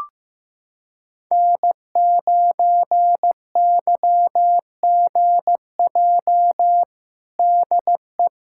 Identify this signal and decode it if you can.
Morse code, heard as a keyed tone, about 15 wpm, 700 Hz. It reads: N9YGJ DE
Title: USA Callsign DE Callsign 15wpm